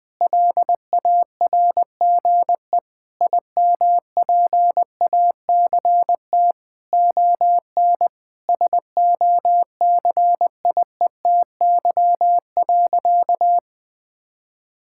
Morse code, heard as a keyed tone, about 19 wpm, 700 Hz. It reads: LARGE IMPACT ON SOCIETY.
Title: Sentences from Top 1000 Words - Review Part 6 - 20wpm